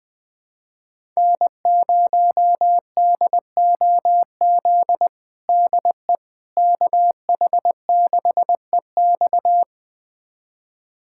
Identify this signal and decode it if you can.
Morse code, heard as a keyed tone, about 21 wpm, 700 Hz. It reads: N0DOZ DE KH6EX
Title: USA Callsign DE Callsign - 5 Characters - 20wpm